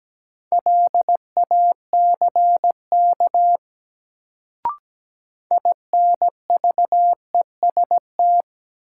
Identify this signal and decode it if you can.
Morse code, heard as a keyed tone, about 17 wpm, 700 Hz. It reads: LACK INVEST
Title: Sentences from Top 1100 Words - New Word Drill Alt - 17wpm